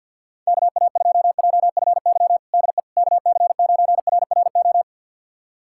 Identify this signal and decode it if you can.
Morse code, heard as a keyed tone, about 51 wpm, 690 Hz. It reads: KA2JFY DE KC9RRO
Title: USA Callsign DE Callsign - 6 Characters - 50wpm